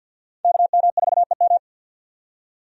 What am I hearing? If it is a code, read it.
Morse code, heard as a keyed tone, about 47 wpm, 700 Hz. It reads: KM4EM
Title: USA Call Signs - 5 Characters - 50wpm